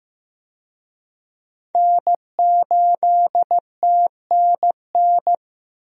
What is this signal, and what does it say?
Morse code, heard as a keyed tone, about 15 wpm, 700 Hz. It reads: N8TNN
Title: USA Call Signs - 5 Characters - 15wpm